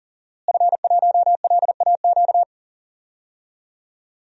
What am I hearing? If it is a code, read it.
Morse code, heard as a keyed tone, about 39 wpm, 700 Hz. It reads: F1LAQ
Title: DXCC-100 Call Signs - 5 Characters - Repeated 2x - 40wpm